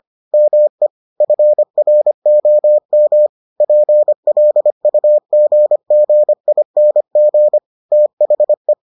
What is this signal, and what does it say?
Morse code, heard as a keyed tone, about 25 wpm, 600 Hz. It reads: ME FROM PLUGGING THE